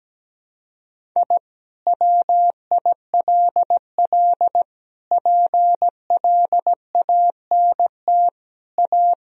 Morse code, heard as a keyed tone, about 17 wpm, 700 Hz. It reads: I WILL PLANT A
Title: Sentences from Top 1000 Words - Review Part 1 - 17wpm